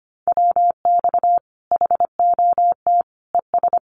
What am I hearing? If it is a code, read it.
Morse code, heard as a keyed tone, about 25 wpm, 700 Hz. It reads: WX HOT ES
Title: Common QSO Phrases 25wpm